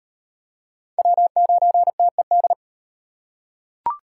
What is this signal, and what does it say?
Morse code, heard as a keyed tone, about 36 wpm, 700 Hz. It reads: W9TED